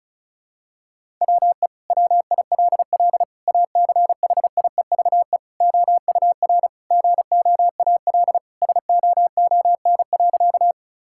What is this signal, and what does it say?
Morse code, heard as a keyed tone, about 36 wpm, 700 Hz. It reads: WE WILL ACHIEVE OUR GOAL SOON.
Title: Sentences from Top 1000 Words - Review Part 6 - 35wpm